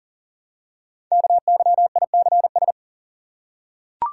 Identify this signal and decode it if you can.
Morse code, heard as a keyed tone, about 39 wpm, 700 Hz. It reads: KYICS
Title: Code Groups - 5 Letters - 40wpm